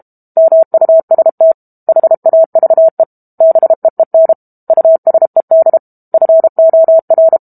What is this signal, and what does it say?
Morse code, heard as a keyed tone, about 32 wpm, 650 Hz. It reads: MUST HAVE BEEN USED FOR
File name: Book Two / 06-Chapter 6 - 32 WPM